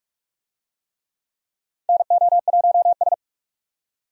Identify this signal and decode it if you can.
Morse code, heard as a keyed tone, about 46 wpm, 700 Hz. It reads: NO1S
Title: USA Call Signs - 4 Characters - 45wpm